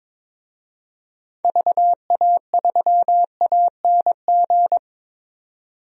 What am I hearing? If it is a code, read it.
Morse code, heard as a keyed tone, about 22 wpm, 700 Hz. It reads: VA3ANG
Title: DXCC-100 Call Signs - 6 Characters - 22wpm